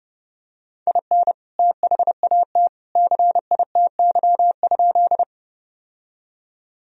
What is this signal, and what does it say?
Morse code, heard as a keyed tone, about 30 wpm, 700 Hz. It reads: IN THAT CITY?
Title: Sentences from Top 1000 Words - Review Part 1 - 30wpm